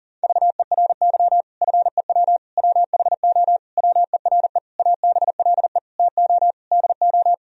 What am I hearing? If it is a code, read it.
Morse code, heard as a keyed tone, about 38 wpm, 700 Hz. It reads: VERY FEW WHO WERE ABLE TO DO
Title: Sentences from Top 1000 Words - Review Part 2 - 40wpm